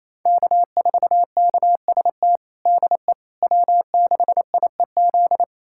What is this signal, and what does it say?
Morse code, heard as a keyed tone, about 28 wpm, 700 Hz. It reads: K4KST DE W6IEZ
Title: USA Callsign DE Callsign - 5 Characters - 28wpm